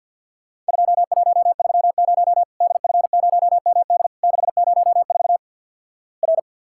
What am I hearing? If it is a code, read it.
Morse code, heard as a keyed tone, about 49 wpm, 700 Hz. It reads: 2130 DF0MD 604 R
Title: Worked All Europe DX Contest - QTC Exchange with Repeats - 50wpm